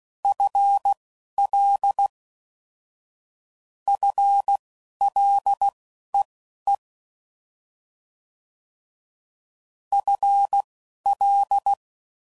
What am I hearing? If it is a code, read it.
Morse code, heard as a keyed tone, about 16 wpm, spaced as for 11 wpm, 800 Hz. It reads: FL FLEE FL
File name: Prefixes / FLs & THs / FL Prefixes 1